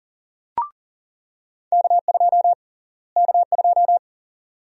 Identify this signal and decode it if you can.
Morse code, heard as a keyed tone, about 40 wpm, 700 Hz. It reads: K2 K2
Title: USA Call Sign Prefixes - Repeated 2x - 40wpm